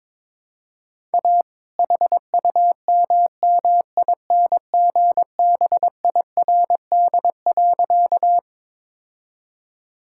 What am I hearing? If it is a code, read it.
Morse code, heard as a keyed tone, about 23 wpm, 700 Hz. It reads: A HUMMINGBIRD.
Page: morse-code-ninja-practice-22wpm